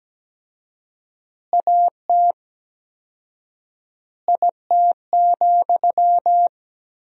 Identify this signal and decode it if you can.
Morse code, heard as a keyed tone, about 17 wpm, 700 Hz. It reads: AT IT,
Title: 4x Word-Spacing - Sentences from Top 200 Words - 17wpm